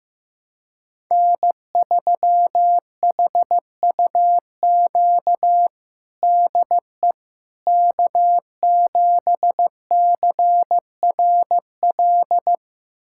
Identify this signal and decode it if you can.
Morse code, heard as a keyed tone, about 15 wpm, 700 Hz. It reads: N3HUQ DE K7CRL
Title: USA Callsign DE Callsign - 5 Characters - 15wpm